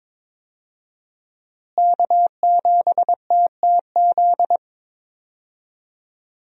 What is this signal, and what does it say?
Morse code, heard as a keyed tone, about 22 wpm, 700 Hz. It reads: K7TTZ
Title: USA Call Signs - 5 Characters - Repeated 3x - 22wpm